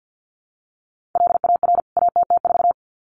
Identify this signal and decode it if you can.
Morse code, heard as a keyed tone, about 50 wpm, 700 Hz. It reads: LAR ATT4
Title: CWT Exchange 50wpm